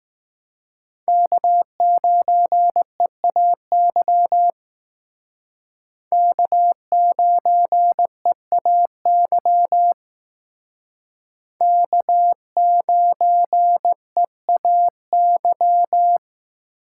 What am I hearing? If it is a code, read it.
Morse code, heard as a keyed tone, about 19 wpm, 700 Hz. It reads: K9EAY K9EAY K9EAY
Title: Speed Racing - USA 1x3 Call Signs 15wpm